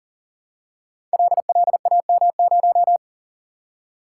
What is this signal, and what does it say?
Morse code, heard as a keyed tone, about 39 wpm, 700 Hz. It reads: LLAM0
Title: Code Groups - 5 Letter:Numbers - 40wpm